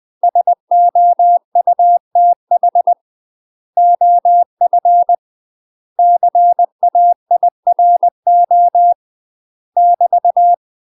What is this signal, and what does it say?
Morse code, heard as a keyed tone, about 20 wpm, 700 Hz. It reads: SOUTH OF CAIRO <BT>
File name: UK MASTER QSOs TRAINING/20WPM